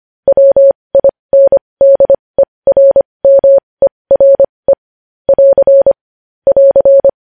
Morse code, heard as a keyed tone, about 25 wpm, spaced as for 20 wpm, 550 Hz. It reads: WINDERMERE <AR> <AR>
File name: podcast / News headlines-1x-25x20